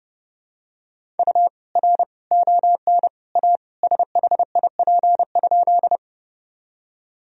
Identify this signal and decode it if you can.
Morse code, heard as a keyed tone, about 30 wpm, 700 Hz. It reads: U R ON A SHIP?
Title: Sentences from Top 400 Words and Abbreviations 30wpm